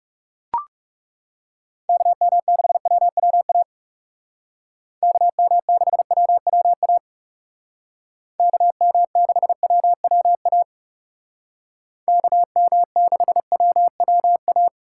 Morse code, heard as a keyed tone, about 36 wpm, 700 Hz. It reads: KM6WWA KM6WWA KM6WWA KM6WWA
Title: Speed Racing - USA 2x3 Call Signs 30wpm